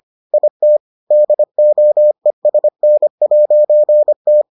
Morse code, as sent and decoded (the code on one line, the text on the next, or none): .. - / -.. --- . ... -. .----. -
IT DOESN'T